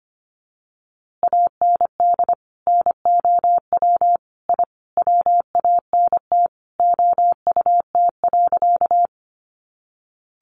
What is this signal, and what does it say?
Morse code, heard as a keyed tone, about 25 wpm, 700 Hz. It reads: AND NOW I WANT OUT.
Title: A Sound Of Thunder 25wpm